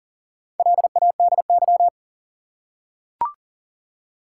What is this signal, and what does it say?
Morse code, heard as a keyed tone, about 40 wpm, 700 Hz. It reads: LADY
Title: Four Letter Words - 40wpm